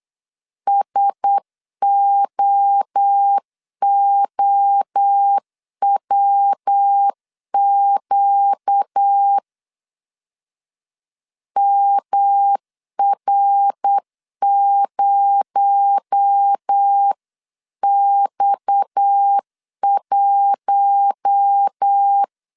Morse code, heard as a keyed tone, about 8 wpm, 800 Hz. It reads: SOOWQ MR0X1
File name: CW / DET / 1 6 1500C S B 8wpm 5142